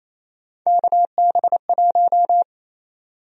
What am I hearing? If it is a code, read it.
Morse code, heard as a keyed tone, about 28 wpm, 700 Hz. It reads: KB1
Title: USA Call Sign Prefixes - 28wpm